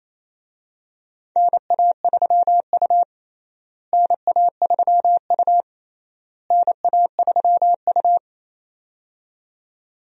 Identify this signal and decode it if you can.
Morse code, heard as a keyed tone, about 28 wpm, 700 Hz. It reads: NA3U NA3U NA3U
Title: USA Call Signs - 4 Characters - Repeated 3x - 28wpm